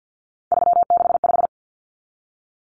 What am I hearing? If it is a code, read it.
Morse code, heard as a keyed tone, about 50 wpm, 700 Hz. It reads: 365
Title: Common Cut Numbers - 3-Digit 50wpm